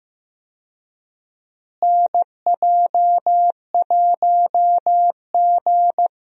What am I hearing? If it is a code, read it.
Morse code, heard as a keyed tone, about 15 wpm, 700 Hz. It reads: NJ1G
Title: USA Call Signs - 4 Characters - Repeated 2x - 15wpm